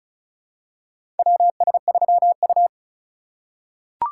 Morse code, heard as a keyed tone, about 36 wpm, 700 Hz. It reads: WS3U
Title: USA Call Signs - 4 Characters - 3x Character Spacing - 35wpm